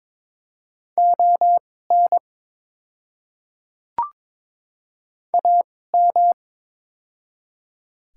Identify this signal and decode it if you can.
Morse code, heard as a keyed tone, about 22 wpm, spaced as for 16 wpm, 700 Hz. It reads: ON AM
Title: 2x Character Spacing - Top 500 Words - 22wpm